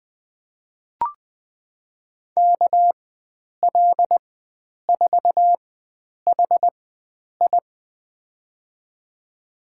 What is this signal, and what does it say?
Morse code, heard as a keyed tone, about 20 wpm, spaced as for 9 wpm, 700 Hz. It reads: KL4HI
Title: USA Call Signs - 5 Characters - 4x Character Spacing - 20wpm